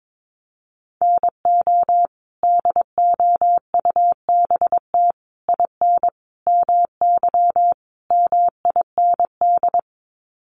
Morse code, heard as a keyed tone, about 22 wpm, 700 Hz. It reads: NO DOUBT IN MY MIND
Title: Sets of 5 Words Encore2 22wpm